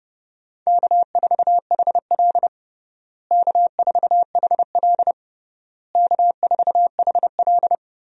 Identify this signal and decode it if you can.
Morse code, heard as a keyed tone, about 30 wpm, 700 Hz. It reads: K4HL K4HL K4HL
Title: USA Call Signs - 4 Characters - Repeated 3x - 30wpm